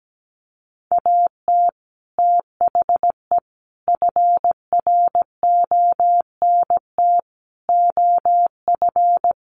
Morse code, heard as a keyed tone, about 17 wpm, 700 Hz. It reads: AT THE FRONT OF
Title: Sets of 5 Words 17wpm